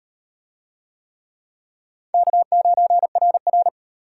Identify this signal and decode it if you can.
Morse code, heard as a keyed tone, about 35 wpm, 700 Hz. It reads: K9RR